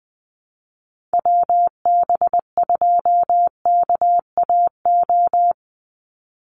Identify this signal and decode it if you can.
Morse code, heard as a keyed tone, about 19 wpm, 700 Hz. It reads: WB2KAO
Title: Call Signs Repeated 3x - 20wpm